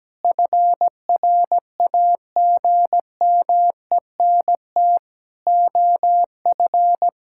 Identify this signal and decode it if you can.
Morse code, heard as a keyed tone, about 17 wpm, 700 Hz. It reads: FRAGMENT OF
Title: Pangram Challenge v2 - 17wpm